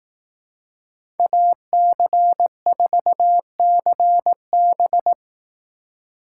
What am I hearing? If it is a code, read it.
Morse code, heard as a keyed tone, about 18 wpm, 700 Hz. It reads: AC4CB
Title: Speed Racing - USA 2x2 Call Signs 15wpm